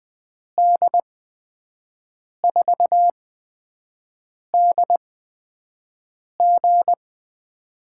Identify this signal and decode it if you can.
Morse code, heard as a keyed tone, about 20 wpm, spaced as for 5 wpm, 700 Hz. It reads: D4DG
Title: USA Call Signs - 8x Character Spacing - 20wpm